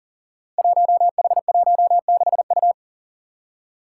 Morse code, heard as a keyed tone, about 39 wpm, 700 Hz. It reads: 1H16U
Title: Five Challenging Letter-Number Combinations 40wpm